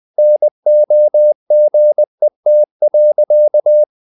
Morse code, heard as a keyed tone, about 20 wpm, 600 Hz. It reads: NOGET.
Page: CW med Gnister